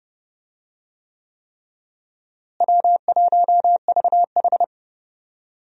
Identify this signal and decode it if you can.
Morse code, heard as a keyed tone, about 29 wpm, 700 Hz. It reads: W1VH
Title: USA Call Signs - 4 Characters - Repeated 3x - 30wpm